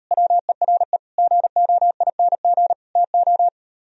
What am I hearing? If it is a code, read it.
Morse code, heard as a keyed tone, about 38 wpm, 700 Hz. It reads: WERE GOING TO